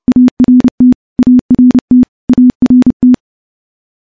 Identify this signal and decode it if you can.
Morse code, heard as a keyed tone, about 29 wpm, 250 Hz. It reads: ART ART ART
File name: morse / morse-training-30WPM